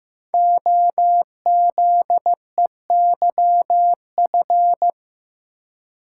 Morse code, heard as a keyed tone, about 15 wpm, 700 Hz. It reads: OZEYF
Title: Code Groups - 5 Letter:Numbers - 15wpm